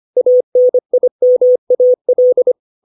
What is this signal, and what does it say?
Morse code, heard as a keyed tone, about 25 wpm, 500 Hz. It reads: ANIMAL
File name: 632 words / 632 Words 25WPM